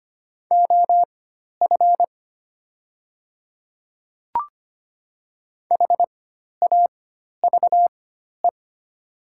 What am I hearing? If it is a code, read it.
Morse code, heard as a keyed tone, about 25 wpm, spaced as for 12 wpm, 690 Hz. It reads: OF HAVE
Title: 4x Character Spacing - Top 100 Words - 25wpm